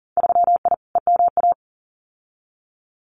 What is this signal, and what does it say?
Morse code, heard as a keyed tone, about 42 wpm, 700 Hz. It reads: 3I EMA
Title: Winter Field Day Exchanges Encore 40wpm